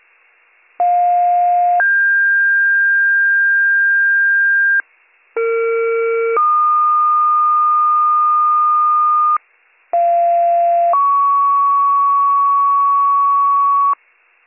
Two-tone page, heard as a one-second tone, then a three-second tone, about 710 then 1600 Hz, about 460 then 1220 Hz, about 690 then 1120 Hz. It is heard from 0.8 to 13.9 s.